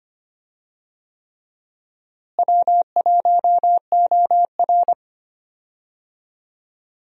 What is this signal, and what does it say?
Morse code, heard as a keyed tone, about 25 wpm, 700 Hz. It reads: W1OR
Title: USA Call Signs - 4 Characters - Repeated 3x - 25wpm